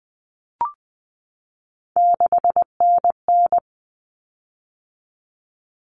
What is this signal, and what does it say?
Morse code, heard as a keyed tone, about 20 wpm, 700 Hz. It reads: 6NN
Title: Common Cut Numbers - 3-Digit 20wpm